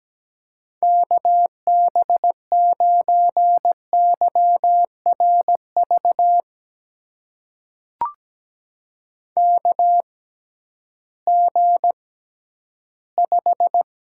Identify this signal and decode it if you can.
Morse code, heard as a keyed tone, about 17 wpm, spaced as for 11 wpm, 700 Hz. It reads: KB9YRV KG5
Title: USA Call Signs - 6 Characters - 6x Character Spacing - 17wpm